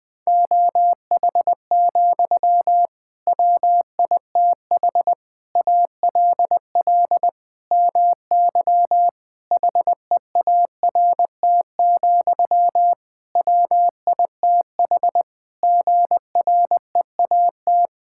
Morse code, heard as a keyed tone, about 20 wpm, 700 Hz. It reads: OH, WITH ALL MY HEART, WITH GREAT
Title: Sentences from Top 1600 Words 20wpm